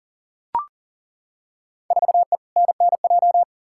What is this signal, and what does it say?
Morse code, heard as a keyed tone, about 40 wpm, 700 Hz. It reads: 4E NNJ
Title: Field Day Exchanges - 40wpm